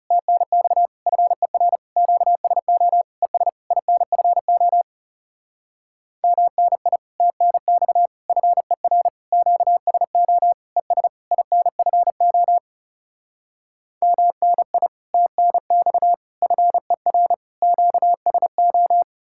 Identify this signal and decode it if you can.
Morse code, heard as a keyed tone, about 40 wpm, 700 Hz. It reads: TNX FER QSO ES INFO MNI TNX FER QSO ES INFO MNI TNX FER QSO